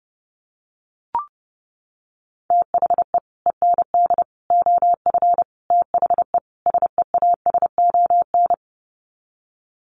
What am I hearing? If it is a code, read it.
Morse code, heard as a keyed tone, about 30 wpm, 700 Hz. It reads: THE END OF THE SEASON
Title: Sets of 5 Words Encore 30wpm